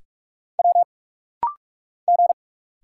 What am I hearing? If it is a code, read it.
Morse code, heard as a keyed tone, about 46 wpm, 700 Hz. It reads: W G
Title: Single Letter-Number Alt - Rapid-Fire 45wpm